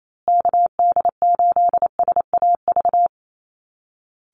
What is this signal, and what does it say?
Morse code, heard as a keyed tone, about 28 wpm, 700 Hz. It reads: KD8SAV
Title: Call Signs Repeated 2x - 28wpm